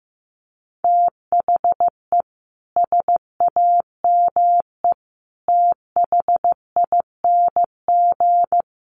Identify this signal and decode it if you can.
Morse code, heard as a keyed tone, about 15 wpm, 700 Hz. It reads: THE SAME THING
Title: Sets of 3 Words 15wpm